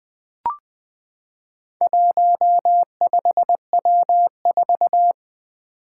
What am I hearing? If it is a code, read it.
Morse code, heard as a keyed tone, about 20 wpm, 700 Hz. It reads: 15W4
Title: Four Challenging Letter-Number Combinations 20wpm